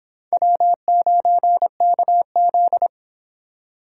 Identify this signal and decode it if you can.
Morse code, heard as a keyed tone, about 30 wpm, 700 Hz. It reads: W9KZ